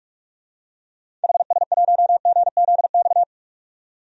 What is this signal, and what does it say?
Morse code, heard as a keyed tone, about 46 wpm, 700 Hz. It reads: HS1GZX
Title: DXCC-100 Call Signs - 6 Characters - 45wpm